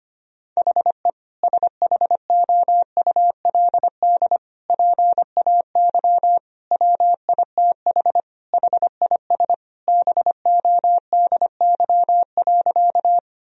Morse code, heard as a keyed tone, about 25 wpm, 700 Hz. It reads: HE SHOULD PAY WITH HIS BODY.
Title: Sentences from Top 1000 Words - Review Part 2 - 25wpm